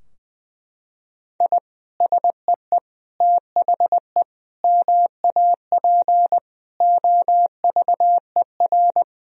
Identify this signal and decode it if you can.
Morse code, heard as a keyed tone, about 20 wpm, 700 Hz. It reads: I SEE THE MAP OVER
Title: Sentences from Top 1000 Words - Review Part 2 - 20wpm